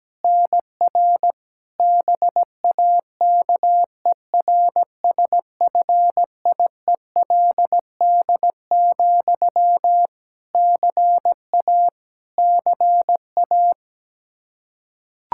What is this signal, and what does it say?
Morse code, heard as a keyed tone, about 17 wpm, 700 Hz. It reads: NR BAKERSFIELD, CA CA
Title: Practice QSO - Exchange 1 - Beginning - 17wpm